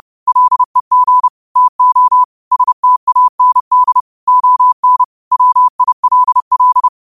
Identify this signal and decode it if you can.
Morse code, heard as a keyed tone, about 30 wpm, 1000 Hz. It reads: LEG TO STAND ON WILL
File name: qotdfiles / 30wpm